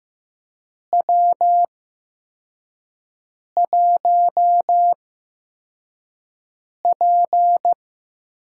Morse code, heard as a keyed tone, about 15 wpm, spaced as for 4 wpm, 700 Hz. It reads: W1P
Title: USA Call Signs - 4 Characters - 8x Character Spacing - 15wpm